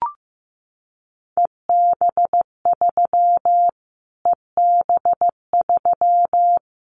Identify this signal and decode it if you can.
Morse code, heard as a keyed tone, about 15 wpm, 700 Hz. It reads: EB3 EB3
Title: Callsign Prefixes for Top 25 DXCC - Repeated 3x - 15wpm